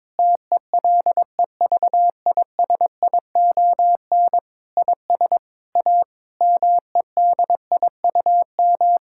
Morse code, heard as a keyed tone, about 22 wpm, 700 Hz. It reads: TELEVISION IS A MEDIUM
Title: Sentences from Top 1000 Words - Review Part 6 - 22wpm